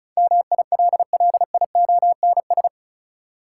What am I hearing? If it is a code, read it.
Morse code, heard as a keyed tone, about 36 wpm, 700 Hz. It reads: MILLIONS
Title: Sets of 5 Words - New Word Drill - 35wpm